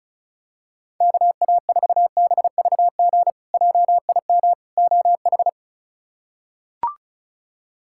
Morse code, heard as a keyed tone, about 34 wpm, 700 Hz. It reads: KA4BVG JIM OH
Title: K1USN SST QSO Practice - 35wpm